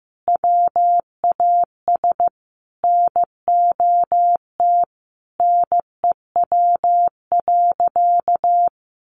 Morse code, heard as a keyed tone, about 15 wpm, 700 Hz. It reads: WAS NOT NEW.
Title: Sentences from Top 1100 Words 15wpm